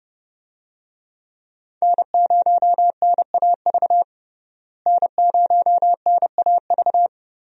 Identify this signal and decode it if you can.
Morse code, heard as a keyed tone, about 29 wpm, 700 Hz. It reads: N0NAV N0NAV
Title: USA Call Signs - 5 Characters - Repeated 3x - 30wpm